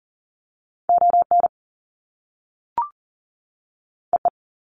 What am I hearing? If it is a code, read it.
Morse code, heard as a keyed tone, about 40 wpm, 700 Hz. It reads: ON EE
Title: Top Two Letter Combinations in Words 40wpm